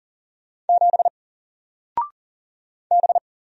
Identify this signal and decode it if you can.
Morse code, heard as a keyed tone, about 40 wpm, 700 Hz. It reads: 7 B
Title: Single Character (Letter:Number:Punctuation:Prosign) Alt 40wpm